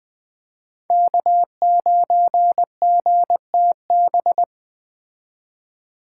Morse code, heard as a keyed tone, about 20 wpm, 700 Hz. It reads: K9GTB